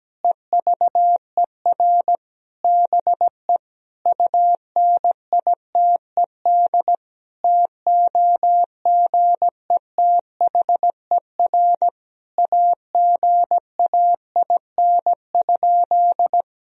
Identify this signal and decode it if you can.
Morse code, heard as a keyed tone, about 17 wpm, 700 Hz. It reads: EVER BE UNITED TOGETHER AGAIN?
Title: Sentences from Top 1000 Words - Review Part 4 - 17wpm